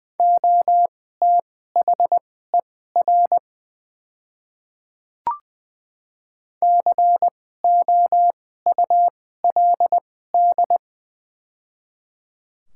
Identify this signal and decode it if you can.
Morse code, heard as a keyed tone, about 20 wpm, spaced as for 14 wpm, 700 Hz. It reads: OTHER COULD
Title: 2x Character Spacing - Top 100 Words - 20wpm